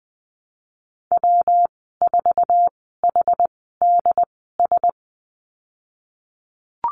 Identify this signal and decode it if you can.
Morse code, heard as a keyed tone, about 20 wpm, spaced as for 15 wpm, 700 Hz. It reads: W4HDS
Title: 147 Lesson 24g - Letter y - Call Signs 20wpm